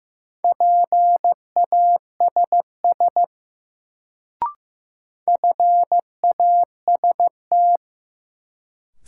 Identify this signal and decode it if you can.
Morse code, heard as a keyed tone, about 15 wpm, 700 Hz. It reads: PASS FAST
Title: Sentences from Top 400 Words - New Word Drill - 15wpm